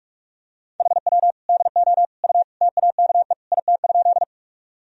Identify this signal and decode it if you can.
Morse code, heard as a keyed tone, about 44 wpm, 700 Hz. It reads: HW DO U TAKE IT?
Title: Sentences from Top 200 Words and Abbreviations 45wpm